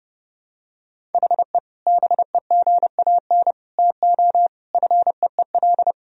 Morse code, heard as a keyed tone, about 30 wpm, 700 Hz. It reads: HE BEGAN TO FEEL
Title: Sentences from Top 1000 Words - Review Part 1 - 30wpm